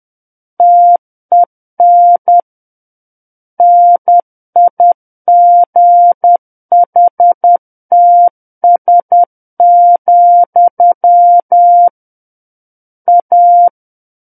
Morse code, heard as a keyed tone, about 10 wpm, 700 Hz. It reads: TEN NIGHTS, A
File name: Book One / 01-Chapter 1 - 10 WPM